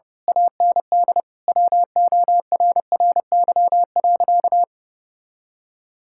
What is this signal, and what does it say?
Morse code, heard as a keyed tone, about 32 wpm, 700 Hz. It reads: AND WORRY.
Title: Pangram Challenge v2 - 30wpm